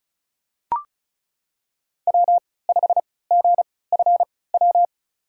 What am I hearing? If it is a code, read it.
Morse code, heard as a keyed tone, about 36 wpm, spaced as for 20 wpm, 700 Hz. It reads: W5GFW
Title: USA Call Signs - 5 Characters - 3x Character Spacing - 35wpm